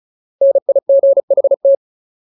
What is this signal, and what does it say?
Morse code, heard as a keyed tone, about 34 wpm, 550 Hz. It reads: NIGHT
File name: Ultimate Word List / 35WPM Ultimate